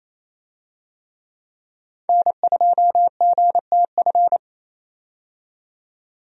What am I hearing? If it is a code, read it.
Morse code, heard as a keyed tone, about 28 wpm, 700 Hz. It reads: N2GTF
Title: USA Call Signs - 5 Characters - 28wpm